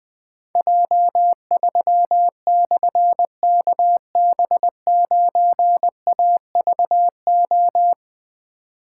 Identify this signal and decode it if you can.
Morse code, heard as a keyed tone, about 21 wpm, 700 Hz. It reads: J3/KB9AVO
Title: Difficult Call Signs 20wpm